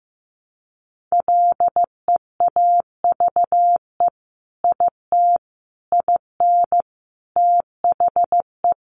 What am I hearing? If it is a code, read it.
Morse code, heard as a keyed tone, about 15 wpm, 700 Hz. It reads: LEAVE IT IN THE
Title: Sentences from Top 1300 Words 15wpm